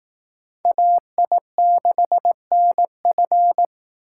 Morse code, heard as a keyed tone, about 18 wpm, 700 Hz. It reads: AI6NF
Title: Speed Racing - USA 2x2 Call Signs 15wpm